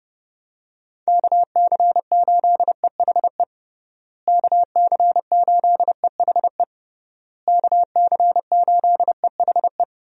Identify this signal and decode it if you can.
Morse code, heard as a keyed tone, about 30 wpm, 700 Hz. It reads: KC8EHE KC8EHE KC8EHE
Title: USA Call Signs - 6 Characters - Repeated 3x - 30wpm